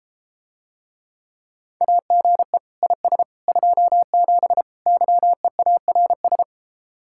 Morse code, heard as a keyed tone, about 32 wpm, 700 Hz. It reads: AGE IS 27 YEARS